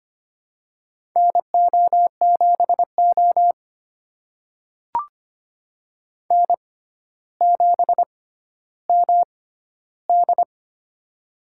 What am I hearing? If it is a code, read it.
Morse code, heard as a keyed tone, about 25 wpm, spaced as for 13 wpm, 700 Hz. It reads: NO7O N7MD
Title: USA Call Signs - 4 Characters - 6x Character Spacing - 25wpm